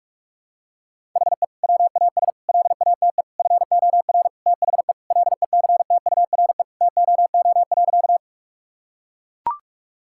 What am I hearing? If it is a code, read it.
Morse code, heard as a keyed tone, about 41 wpm, 700 Hz. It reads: HE WAS LATE FOR THE LECTURE TOO.
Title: Sentences from Top 1600 Words 45wpm